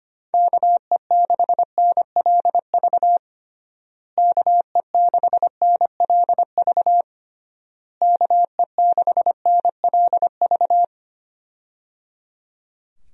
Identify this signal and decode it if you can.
Morse code, heard as a keyed tone, about 25 wpm, 700 Hz. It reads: KE6NLV KE6NLV KE6NLV
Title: USA Call Signs - 6 Characters - Repeated 3x - 25wpm